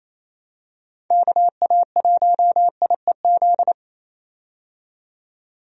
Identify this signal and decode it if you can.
Morse code, heard as a keyed tone, about 27 wpm, 700 Hz. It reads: KA1IEZ
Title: USA Call Signs - 6 Characters - 28wpm